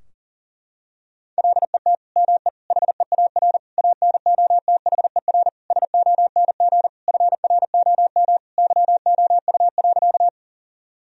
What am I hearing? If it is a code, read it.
Morse code, heard as a keyed tone, about 40 wpm, 700 Hz. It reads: LET ME HEAR ANOTHER SONG FROM YOU.
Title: Sentences from Top 1000 Words - Review Part 2 - 40wpm